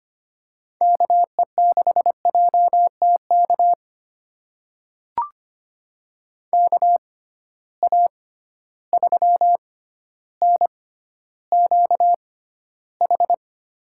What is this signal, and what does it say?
Morse code, heard as a keyed tone, about 25 wpm, spaced as for 13 wpm, 700 Hz. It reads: KE6JTK KA3NQH
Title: USA Call Signs - 6 Characters - 6x Character Spacing - 25wpm